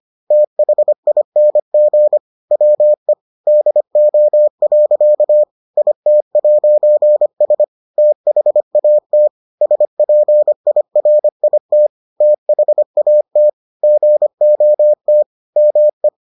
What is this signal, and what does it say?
Morse code, heard as a keyed tone, about 25 wpm, 600 Hz. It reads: THING WE DO. IT'S THAT SPIRIT THAT GOT ME